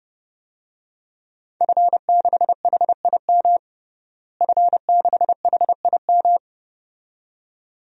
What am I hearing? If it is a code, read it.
Morse code, heard as a keyed tone, about 32 wpm, 700 Hz. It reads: F6HIM F6HIM
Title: DXCC-100 Call Signs - 5 Characters - Repeated 2x - 30wpm